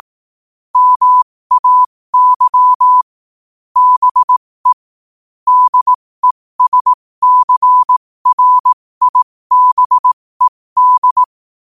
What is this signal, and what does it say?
Morse code, heard as a keyed tone, about 18 wpm, spaced as for 15 wpm, 1000 Hz. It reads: MAY BE DESCRIBED
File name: qotdfiles / 15wpm